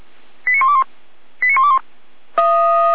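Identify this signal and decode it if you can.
Five-tone selective call (ZVEI), heard as quick runs of five tones, tones of about 70 ms. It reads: 89121 89121
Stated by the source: ZVEI five-tone selective call received by SDR.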